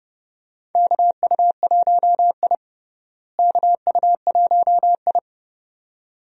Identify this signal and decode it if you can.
Morse code, heard as a keyed tone, about 30 wpm, 700 Hz. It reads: KU1I KU1I
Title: USA Call Signs - 4 Characters - Repeated 2x - 30wpm